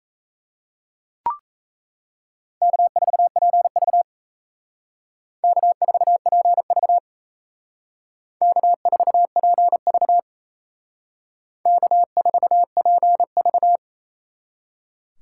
Morse code, heard as a keyed tone, about 35 wpm, 700 Hz. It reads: K4PV K4PV K4PV K4PV
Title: Speed Racing - USA 1x2 Call Signs 28wpm